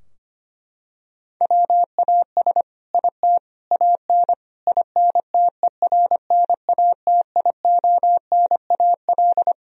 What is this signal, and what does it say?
Morse code, heard as a keyed tone, about 25 wpm, 700 Hz. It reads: WAS IT AN INTERNATIONAL
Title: Sentences from Top 1000 Words - Review Part 4 - 25wpm